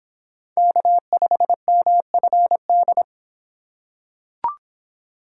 Morse code, heard as a keyed tone, about 25 wpm, 700 Hz. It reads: K5MFD